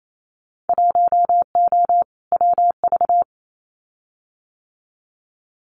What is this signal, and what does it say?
Morse code, heard as a keyed tone, about 28 wpm, 700 Hz. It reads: 1O WV
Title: Winter Field Day Exchanges 28wpm